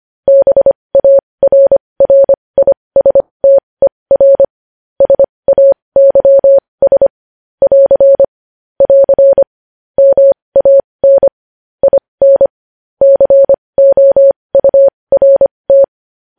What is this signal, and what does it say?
Morse code, heard as a keyed tone, about 25 wpm, spaced as for 20 wpm, 550 Hz. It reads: BARRISTER SAYS <AR> <AR> MAN IN COURT
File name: podcast / News headlines-1x-25x20